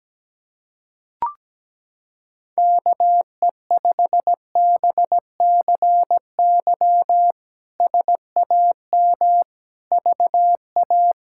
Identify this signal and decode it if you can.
Morse code, heard as a keyed tone, about 17 wpm, 700 Hz. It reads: KE5BCY SAM VA
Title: K1USN SST QSO Practice - 17wpm